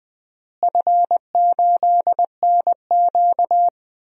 Morse code, heard as a keyed tone, about 21 wpm, 700 Hz. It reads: F8NQ
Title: DXCC-100 Call Signs - 4 Characters - Repeated 3x - 20wpm